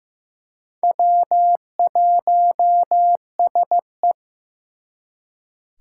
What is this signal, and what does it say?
Morse code, heard as a keyed tone, about 18 wpm, 700 Hz. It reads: W1SE
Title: Speed Racing - USA 1x2 Call Signs 15wpm